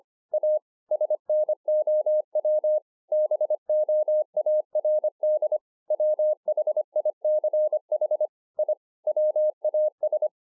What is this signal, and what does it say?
Morse code, heard as a keyed tone, about 25 wpm, 610 Hz. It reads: A SNOW BOARD WHICH I WAS